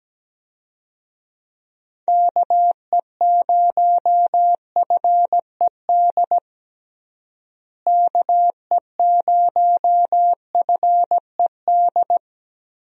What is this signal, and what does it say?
Morse code, heard as a keyed tone, about 17 wpm, 700 Hz. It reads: KE0FED KE0FED
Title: USA Call Signs - 6 Characters - Repeated 3x - 17wpm